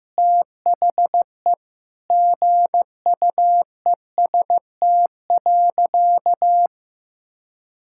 Morse code, heard as a keyed tone, about 15 wpm, 700 Hz. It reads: THE GUEST.
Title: Sentences from Top 1600 Words 15wpm